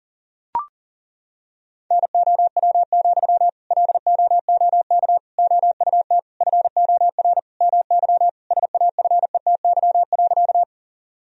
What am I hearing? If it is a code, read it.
Morse code, heard as a keyed tone, about 40 wpm, 700 Hz. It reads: NOW, LOOK OUT FOR MY SAFETY.
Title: Sentences from Top 1000 Words - Review Part 4 - 40wpm